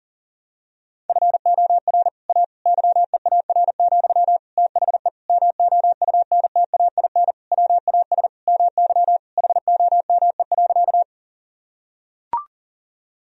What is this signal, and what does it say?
Morse code, heard as a keyed tone, about 40 wpm, 700 Hz. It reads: FOR A YEAR, THE MOUNTAIN WAS MY HOME.
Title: Sentences from Top 1000 Words - Review Part 1 - 40wpm